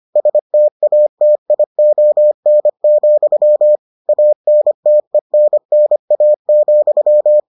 Morse code, heard as a keyed tone, about 25 wpm, 600 Hz. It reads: STATION, ANTENNA,